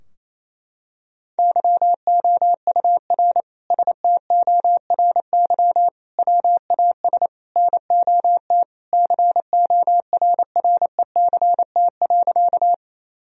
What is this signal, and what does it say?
Morse code, heard as a keyed tone, about 28 wpm, 700 Hz. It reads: YOUR STORY WAS NOT CORRECT.
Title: Sentences from Top 1000 Words - Review Part 2 - 28wpm